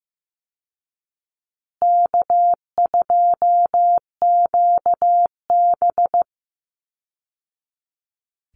Morse code, heard as a keyed tone, about 15 wpm, 700 Hz. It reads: K2QB
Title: Easy Call Signs Repeated 3x - 15wpm